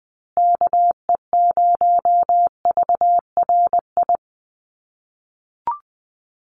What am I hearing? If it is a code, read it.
Morse code, heard as a keyed tone, about 20 wpm, 700 Hz. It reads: KE0VRI
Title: USA Call Signs Repeated 2x 20wpm